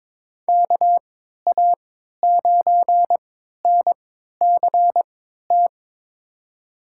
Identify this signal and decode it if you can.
Morse code, heard as a keyed tone, about 22 wpm, spaced as for 13 wpm, 700 Hz. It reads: KA9NCT
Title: USA Call Signs - 6 Characters - 3x Character Spacing - 22wpm